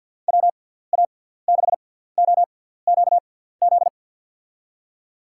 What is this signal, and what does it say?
Morse code, heard as a keyed tone, about 53 wpm, spaced as for 17 wpm, 700 Hz. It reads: WA6OQZ
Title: USA Call Signs - 6 Characters - 6x Character Spacing - 50wpm